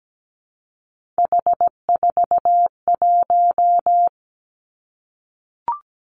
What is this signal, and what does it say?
Morse code, heard as a keyed tone, about 17 wpm, 700 Hz. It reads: H41
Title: Three Challenging Letter-Number Combinations 17wpm